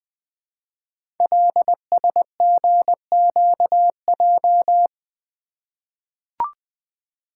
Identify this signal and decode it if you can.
Morse code, heard as a keyed tone, about 20 wpm, 700 Hz. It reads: LSGQJ
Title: Code Groups - 5 Letter:Numbers - 20wpm